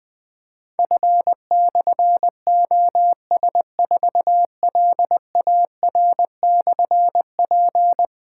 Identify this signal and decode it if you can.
Morse code, heard as a keyed tone, about 20 wpm, 700 Hz. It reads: F/OS4LAR/P
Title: Difficult Call Signs 20wpm